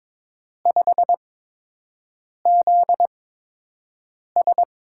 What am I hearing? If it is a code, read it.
Morse code, heard as a keyed tone, about 22 wpm, spaced as for 6 wpm, 700 Hz. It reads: 5ZS
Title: USA Call Signs - 6 Characters - 8x Character Spacing - 22wpm